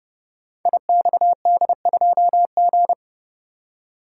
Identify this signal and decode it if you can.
Morse code, heard as a keyed tone, about 30 wpm, 700 Hz. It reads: IXD2G
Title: Code Groups - 5 Letter:Numbers - 30wpm